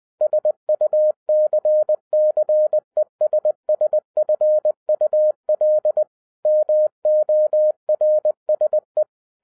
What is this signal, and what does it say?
Morse code, heard as a keyed tone, about 20 wpm, 610 Hz. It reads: SUCCESSFUL MORSE